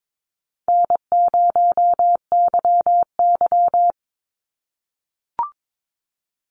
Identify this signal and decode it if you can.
Morse code, heard as a keyed tone, about 22 wpm, 700 Hz. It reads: N0YY
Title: Easy Call Signs - 22wpm